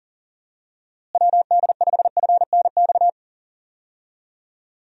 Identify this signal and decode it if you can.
Morse code, heard as a keyed tone, about 37 wpm, 700 Hz. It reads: WD5FNX
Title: USA Call Signs - 6 Characters - Repeated 2x - 40wpm